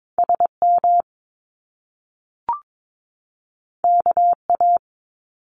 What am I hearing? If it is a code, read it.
Morse code, heard as a keyed tone, about 22 wpm, 700 Hz. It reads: SM KA
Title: Top Two Letter Combinations in Words 22wpm